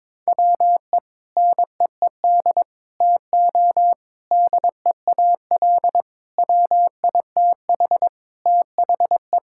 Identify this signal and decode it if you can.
Morse code, heard as a keyed tone, about 22 wpm, 700 Hz. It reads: WE NEED TO DEAL WITH THE
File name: practice / 22wpm / Sentences with 4-Word Phrases - 22wpm